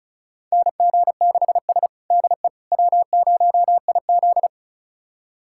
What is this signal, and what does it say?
Morse code, heard as a keyed tone, about 36 wpm, 700 Hz. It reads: NG6S DE W0IZ
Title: USA Callsign DE Callsign - 4 Characters - 35wpm